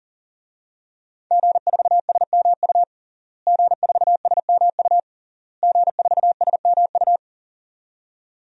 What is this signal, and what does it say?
Morse code, heard as a keyed tone, about 40 wpm, 700 Hz. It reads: G4SMU G4SMU G4SMU
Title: DXCC-100 Call Signs - 5 Characters - Repeated 3x - 40wpm